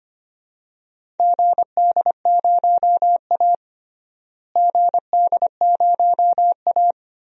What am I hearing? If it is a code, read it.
Morse code, heard as a keyed tone, about 25 wpm, 700 Hz. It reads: GD0A GD0A
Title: DXCC-100 Call Signs - 4 Characters - Repeated 2x - 25wpm